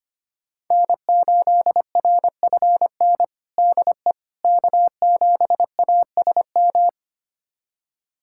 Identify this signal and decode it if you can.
Morse code, heard as a keyed tone, about 25 wpm, 700 Hz. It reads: N8RFN DE K7ASM
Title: USA Callsign DE Callsign - 5 Characters - 25wpm